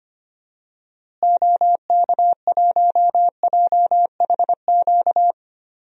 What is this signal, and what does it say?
Morse code, heard as a keyed tone, about 25 wpm, 700 Hz. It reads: OK1JHQ
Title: DXCC-100 Call Signs - 6 Characters - Repeated 2x - 25wpm